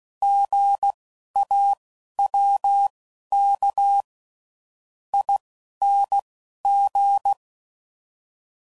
Morse code, heard as a keyed tone, about 16 wpm, spaced as for 11 wpm, 800 Hz. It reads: GAWK ING
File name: Prefixes / By First Letter